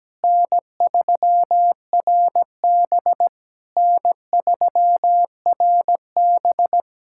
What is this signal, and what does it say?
Morse code, heard as a keyed tone, about 17 wpm, 700 Hz. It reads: N3RB N3RB
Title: SOTA QSO Practice 17wpm